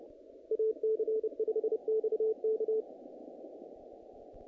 Morse code, heard as a keyed tone, about 27 wpm, 440 Hz. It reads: AC5XK
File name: contests / naqp-cw-k5tr-op-k5pi- -2100